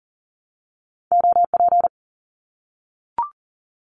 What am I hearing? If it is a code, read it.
Morse code, heard as a keyed tone, about 42 wpm, 700 Hz. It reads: OP
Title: Abbreviations 40wpm